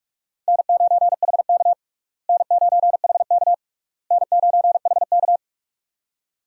Morse code, heard as a keyed tone, about 42 wpm, 700 Hz. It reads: N9HK N9HK N9HK
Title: USA Call Signs - 4 Characters - Repeated 3x - 45wpm